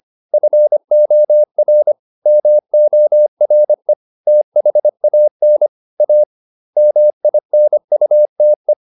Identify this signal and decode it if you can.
Morse code, heard as a keyed tone, about 25 wpm, 600 Hz. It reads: FOR MORE THAN A MINUTE
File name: the old man and the sea part 1 25wpm